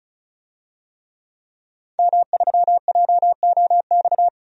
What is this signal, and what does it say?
Morse code, heard as a keyed tone, about 34 wpm, 700 Hz. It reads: M3JOX
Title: DXCC-100 Call Signs - 5 Characters - 35wpm